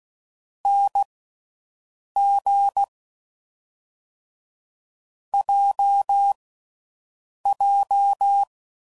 Morse code, heard as a keyed tone, about 16 wpm, spaced as for 6 wpm, 800 Hz. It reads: NG JJ